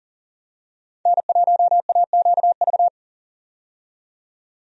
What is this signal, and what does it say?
Morse code, heard as a keyed tone, about 40 wpm, 700 Hz. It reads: N1AQV
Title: USA Call Signs - 5 Characters - 40wpm